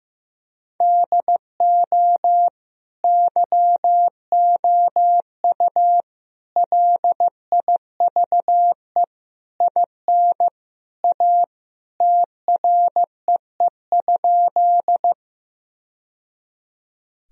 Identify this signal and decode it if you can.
Morse code, heard as a keyed tone, about 15 wpm, 700 Hz. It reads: DO YOU LIVE IN A TREE?
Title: Sentences from Top 1000 Words - Review Part 1 - 15wpm